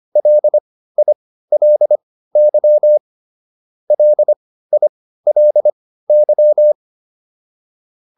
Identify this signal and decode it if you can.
Morse code, heard as a keyed tone, about 25 wpm, spaced as for 15 wpm, 600 Hz. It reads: LILY LILY
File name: LY4 15